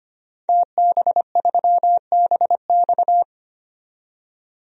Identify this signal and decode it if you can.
Morse code, heard as a keyed tone, about 25 wpm, 700 Hz. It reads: TB3BX
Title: DXCC-100 Call Signs - 5 Characters - 25wpm